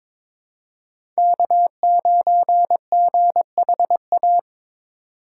Morse code, heard as a keyed tone, about 22 wpm, 700 Hz. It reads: K9GHA
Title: USA Call Signs - 5 Characters - 4x Character Spacing - 22wpm